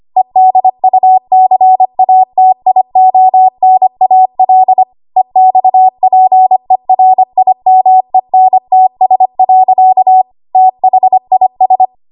Morse code, heard as a keyed tone, about 25 wpm, 750 Hz. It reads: EDUCATIONAL EXPERIMENTS. THIS
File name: Archive / 25 WPM / 25WPM